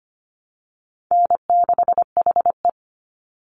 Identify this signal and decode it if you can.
Morse code, heard as a keyed tone, about 25 wpm, 700 Hz. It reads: N6HE
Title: Call Signs Repeated 3x - 25wpm